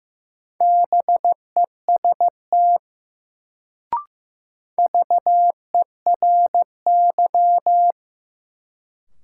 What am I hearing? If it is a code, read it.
Morse code, heard as a keyed tone, about 15 wpm, 700 Hz. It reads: BEST VERY
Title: Sets of 2 Words - New Word Drill - 15wpm